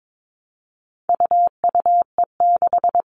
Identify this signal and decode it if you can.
Morse code, heard as a keyed tone, about 22 wpm, 700 Hz. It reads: UUE6
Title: All Cut Numbers - 4-Digit 22wpm